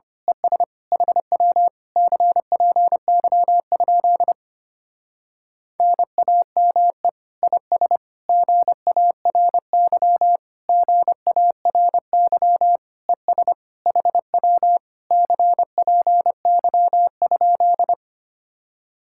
Morse code, heard as a keyed tone, about 24 wpm, 700 Hz. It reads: ES HW CPY? NAME IS GARY GARY ES HW CPY?
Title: Speed-Racing Common QSO Phrases - 25wpm